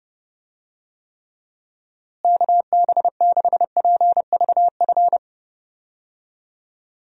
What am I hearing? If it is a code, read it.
Morse code, heard as a keyed tone, about 30 wpm, 700 Hz. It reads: KB6PVF
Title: USA Call Signs - 6 Characters - Repeated 2x - 30wpm